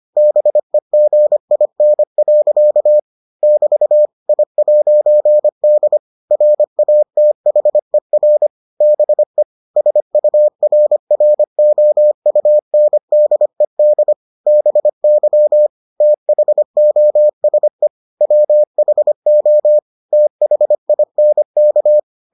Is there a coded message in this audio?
Yes: Morse code from 0.2 to 22.0 s, heard as a keyed tone, about 25 wpm, 600 Hz. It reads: BEGIN. <BT> I'D RATHER BE SURROUNDED BY THOSE WHO THINK